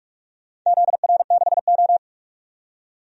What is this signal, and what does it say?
Morse code, heard as a keyed tone, about 43 wpm, 700 Hz. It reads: ZR6O
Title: DXCC-100 Call Signs - 4 Characters - 45wpm